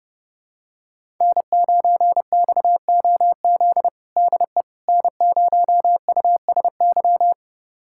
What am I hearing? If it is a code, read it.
Morse code, heard as a keyed tone, about 32 wpm, 700 Hz. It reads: N9XOZ DE N0USY
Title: USA Callsign DE Callsign - 5 Characters - 30wpm